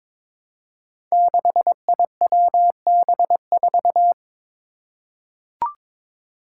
Morse code, heard as a keyed tone, about 22 wpm, 700 Hz. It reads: 6IWB4
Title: Five Challenging Letter-Number Combinations 22wpm